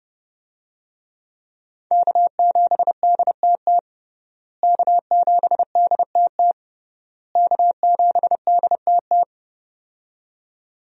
Morse code, heard as a keyed tone, about 30 wpm, 700 Hz. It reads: K7DTT K7DTT K7DTT
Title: USA Call Signs - 5 Characters - Repeated 3x - 30wpm